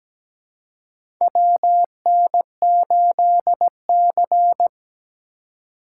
Morse code, heard as a keyed tone, about 17 wpm, 700 Hz. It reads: WN8C
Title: USA Call Signs - 4 Characters - 2x Character Spacing - 17wpm